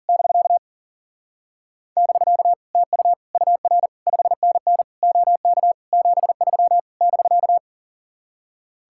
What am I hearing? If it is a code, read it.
Morse code, heard as a keyed tone, about 51 wpm, 700 Hz. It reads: <BK> <BK> TU UR 5NN OK 73 <BK>